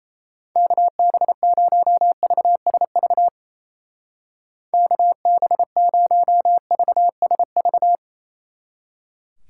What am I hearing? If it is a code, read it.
Morse code, heard as a keyed tone, about 35 wpm, 700 Hz. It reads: KB0VSV KB0VSV
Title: Speed Racing - USA Call Signs 28wpm